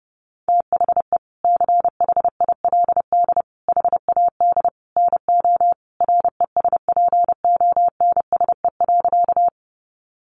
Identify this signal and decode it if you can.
Morse code, heard as a keyed tone, about 30 wpm, 700 Hz. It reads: THE CHILD HAD NO RESPONSE.
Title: Sentences from Top 1400 Words 30wpm